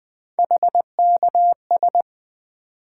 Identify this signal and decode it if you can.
Morse code, heard as a keyed tone, about 20 wpm, 700 Hz. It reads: HKS
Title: Three Letter-Number Combinations - 20wpm